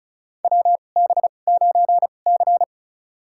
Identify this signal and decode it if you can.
Morse code, heard as a keyed tone, about 36 wpm, spaced as for 25 wpm, 700 Hz. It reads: WB9C
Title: USA Call Signs - 4 Characters - 2x Character Spacing - 35wpm